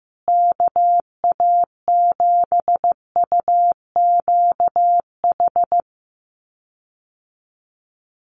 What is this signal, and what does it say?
Morse code, heard as a keyed tone, about 15 wpm, 700 Hz. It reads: KA7UQH
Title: USA Call Signs Repeated 2x 15wpm